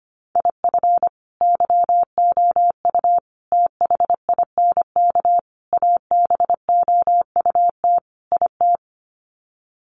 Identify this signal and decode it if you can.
Morse code, heard as a keyed tone, about 25 wpm, 700 Hz. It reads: IF YOU THINK ABOUT IT
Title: Sets of 5 Words Encore2 25wpm